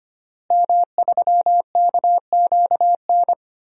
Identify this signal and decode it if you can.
Morse code, heard as a keyed tone, about 25 wpm, 700 Hz. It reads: M3KQN
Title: DXCC-100 Call Signs - 5 Characters - Repeated 3x - 25wpm